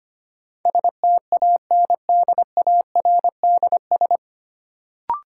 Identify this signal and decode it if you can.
Morse code, heard as a keyed tone, about 25 wpm, 700 Hz. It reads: STANDARDS
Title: Sentences from Top 1000 Words - Review Part 6 - New Word Drill - 25wpm